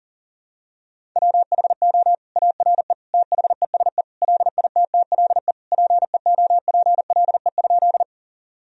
Morse code, heard as a keyed tone, about 37 wpm, 700 Hz. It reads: WHO ARE THESE LITTLE PEOPLE?
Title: Sentences from Top 1000 Words - Review Part 1 - 40wpm